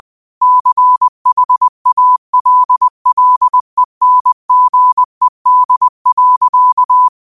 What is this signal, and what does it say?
Morse code, heard as a keyed tone, about 20 wpm, 1000 Hz. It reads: CHALLENGED.